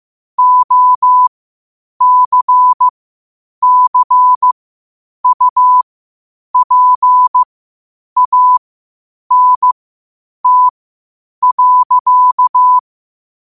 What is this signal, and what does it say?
Morse code, heard as a keyed tone, about 15 wpm, spaced as for 8 wpm, 1000 Hz. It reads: OCCUPANT.
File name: qotdfiles / 8wpm